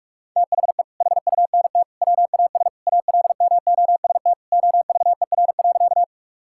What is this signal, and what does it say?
Morse code, heard as a keyed tone, about 46 wpm, 700 Hz. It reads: THE HUNT WAS ALMOST OVER.
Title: Sentences from Top 1600 Words 45wpm